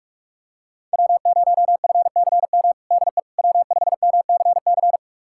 Morse code, heard as a keyed tone, about 46 wpm, 700 Hz. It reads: W0FCM DE W5MCC
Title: USA Callsign DE Callsign - 5 Characters - 45wpm